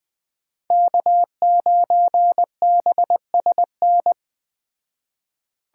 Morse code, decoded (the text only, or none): K9BSN